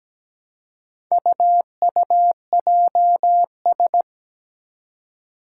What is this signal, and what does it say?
Morse code, heard as a keyed tone, about 17 wpm, 700 Hz. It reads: UUJS
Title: Four Challenging Letter-Number Combinations 17wpm